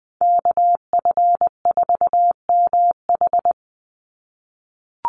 Morse code, heard as a keyed tone, about 20 wpm, 700 Hz. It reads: KF4MH
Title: NA SOTA Chaser Callsigns 20wpm